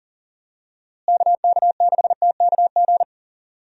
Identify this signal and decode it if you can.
Morse code, heard as a keyed tone, about 40 wpm, 700 Hz. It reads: KK6TKG
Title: USA Call Signs - 6 Characters - Repeated 3x - 40wpm